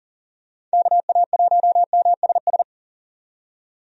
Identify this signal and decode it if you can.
Morse code, heard as a keyed tone, about 40 wpm, 700 Hz. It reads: KA1MSS
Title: USA Call Signs - 6 Characters - 40wpm